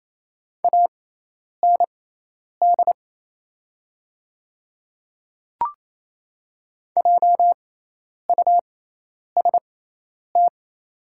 Morse code, heard as a keyed tone, about 28 wpm, spaced as for 10 wpm, 700 Hz. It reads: AND JUST
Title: 6x Character Spacing - Top 200 Words - 28wpm